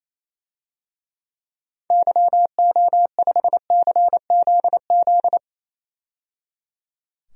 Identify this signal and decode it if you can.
Morse code, heard as a keyed tone, about 27 wpm, 700 Hz. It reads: YO5CZZ
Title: DXCC-100 Call Signs - 6 Characters - 28wpm